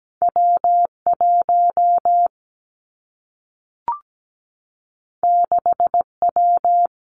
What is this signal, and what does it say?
Morse code, heard as a keyed tone, about 17 wpm, 700 Hz. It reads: W1 6W
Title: Two Challenging Letter-Number Combinations 17wpm